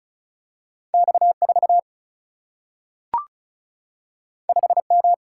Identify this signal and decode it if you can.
Morse code, heard as a keyed tone, about 36 wpm, 700 Hz. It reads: X 4 5 M
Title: Single-Letter Single-Number 35wpm